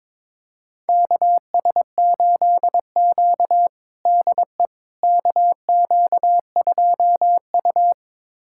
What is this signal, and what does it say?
Morse code, heard as a keyed tone, about 22 wpm, 700 Hz. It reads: KS8Q DE KQ2U
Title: USA Callsign DE Callsign - 4 Characters - 22wpm